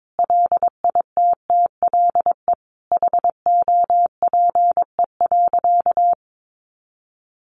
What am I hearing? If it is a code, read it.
Morse code, heard as a keyed tone, about 22 wpm, 700 Hz. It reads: LITTLE HOPE.
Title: Sentences from Top 1000 Words 22wpm